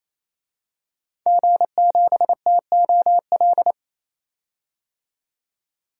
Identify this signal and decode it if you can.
Morse code, heard as a keyed tone, about 28 wpm, 700 Hz. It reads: G7TOL
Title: DXCC-100 Call Signs - 5 Characters - 28wpm